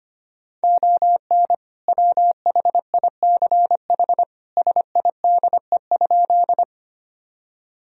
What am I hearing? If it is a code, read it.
Morse code, heard as a keyed tone, about 25 wpm, 700 Hz. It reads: ON WHICH SIDE?
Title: Sentences from Top 1000 Words - Review Part 1 - 25wpm